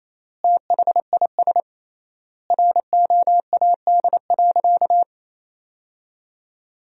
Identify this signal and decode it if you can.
Morse code, heard as a keyed tone, about 28 wpm, 700 Hz. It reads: THIS ROAD.
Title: 3x Word-Spacing - Sentences from Top 400 Words - 28wpm